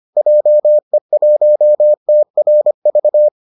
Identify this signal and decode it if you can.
Morse code, heard as a keyed tone, about 25 wpm, 600 Hz. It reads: JE1TRV